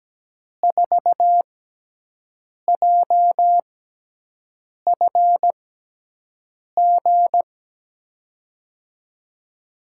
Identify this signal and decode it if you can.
Morse code, heard as a keyed tone, about 17 wpm, spaced as for 6 wpm, 700 Hz. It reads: 4JFG
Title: USA Call Signs - 6 Characters - 6x Character Spacing - 17wpm